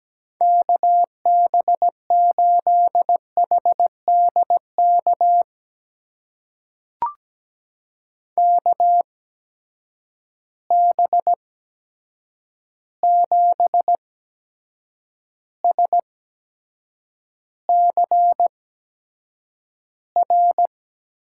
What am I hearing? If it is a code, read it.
Morse code, heard as a keyed tone, about 17 wpm, spaced as for 7 wpm, 700 Hz. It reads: KB8HDK KB7SCR
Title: USA Call Signs - 8x Character Spacing - 17wpm